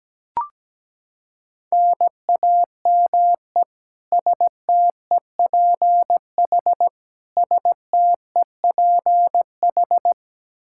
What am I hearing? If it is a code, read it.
Morse code, heard as a keyed tone, about 17 wpm, 700 Hz. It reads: NAME STEPH STEPH
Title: Practice QSO - Exchange 1 - Beginning - 17wpm